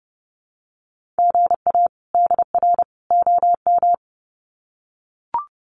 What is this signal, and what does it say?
Morse code, heard as a keyed tone, about 30 wpm, 700 Hz. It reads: GA DR OM
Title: Common QSO Phrases 30wpm